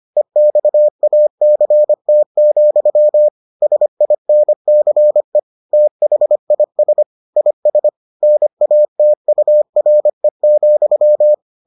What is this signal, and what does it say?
Morse code, heard as a keyed tone, about 25 wpm, 600 Hz. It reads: EXACT, SINCE THIS IS NATURE,